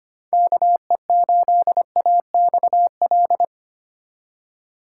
Morse code, heard as a keyed tone, about 25 wpm, 700 Hz. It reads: KE8AXL
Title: USA Call Signs - 6 Characters - 2x Character Spacing - 25wpm